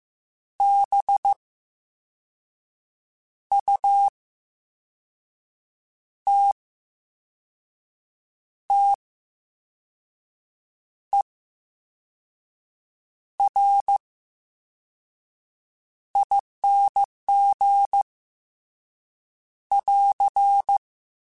Morse code, heard as a keyed tone, about 15 wpm, spaced as for 5 wpm, 800 Hz. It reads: BUTTERING <AR>
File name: Special INGs / Special INGs-mp3s / Slow ing @13